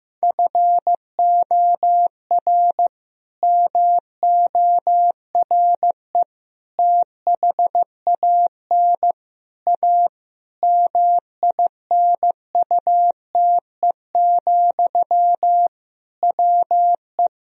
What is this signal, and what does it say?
Morse code, heard as a keyed tone, about 16 wpm, 700 Hz. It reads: FOR MORE THAN A MINUTE, WE
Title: Sentences from Top 1000 Words - Review Part 2 - 15wpm